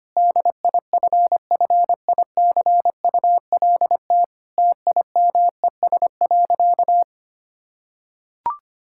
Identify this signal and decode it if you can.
Morse code, heard as a keyed tone, about 25 wpm, 700 Hz. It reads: DIFFICULT TIMES.
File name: practice / 25wpm / Difficult Sentences with Hints - 25wpm